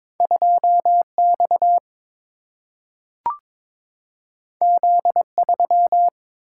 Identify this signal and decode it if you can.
Morse code, heard as a keyed tone, about 22 wpm, 700 Hz. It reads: 2 X Z 3
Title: Single-Letter Single-Number 22wpm